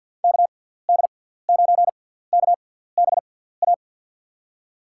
Morse code, heard as a keyed tone, about 53 wpm, spaced as for 17 wpm, 700 Hz. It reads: KD9KBA
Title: USA Call Signs - 6 Characters - 6x Character Spacing - 50wpm